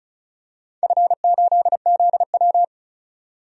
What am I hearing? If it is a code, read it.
Morse code, heard as a keyed tone, about 36 wpm, 700 Hz. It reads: F8ZW
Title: DXCC-100 Call Signs - 4 Characters - 35wpm